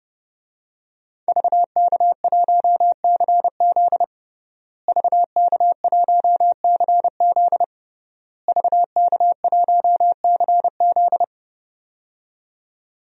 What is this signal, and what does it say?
Morse code, heard as a keyed tone, about 30 wpm, 700 Hz. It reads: VK1CZ VK1CZ VK1CZ
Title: DXCC-100 Call Signs - 5 Characters - Repeated 3x - 30wpm